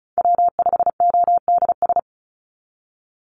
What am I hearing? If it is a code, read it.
Morse code, heard as a keyed tone, about 34 wpm, 700 Hz. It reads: W5ODS
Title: NA SOTA Activator Callsigns 35wpm